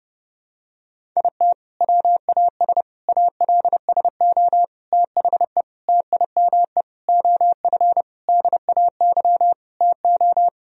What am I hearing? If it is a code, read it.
Morse code, heard as a keyed tone, about 30 wpm, 700 Hz. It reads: IT WAS ALSO THE TIME OF DAY TO
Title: Sentences from Top 1000 Words - Review Part 1 - 30wpm